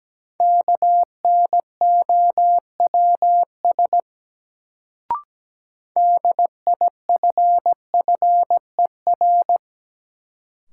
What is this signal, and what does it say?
Morse code, heard as a keyed tone, about 17 wpm, 700 Hz. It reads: KNOWS DIFFER
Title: Sentences from Top 1000 Words - Review Part 1 - New Word Drill - 17wpm